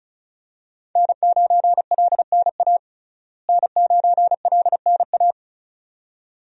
Morse code, heard as a keyed tone, about 36 wpm, 700 Hz. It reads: N9LNA N9LNA
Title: USA Call Signs - 5 Characters - Repeated 2x - 35wpm